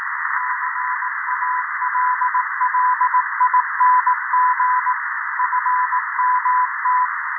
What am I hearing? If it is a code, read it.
Morse code, heard as a keyed tone, about 18 wpm, 1000 Hz, with radio noise. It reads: CALLING FMT
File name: 0340z-fmt k5cm 80m by pa0pje